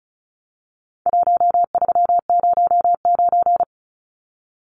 Morse code, heard as a keyed tone, about 34 wpm, 700 Hz. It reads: 1309
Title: Numbers - 4-Digit 35wpm